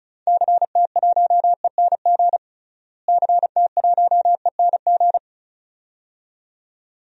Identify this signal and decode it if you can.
Morse code, heard as a keyed tone, about 36 wpm, 700 Hz. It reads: CT1ENG CT1ENG
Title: DXCC-100 Call Signs - 6 Characters - Repeated 2x - 35wpm